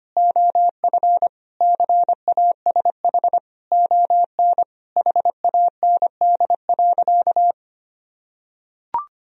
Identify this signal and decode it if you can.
Morse code, heard as a keyed tone, about 25 wpm, 700 Hz. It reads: OF CASH ON HAND.
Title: Sentences from Top 1000 Words - Review Part 4 - 25wpm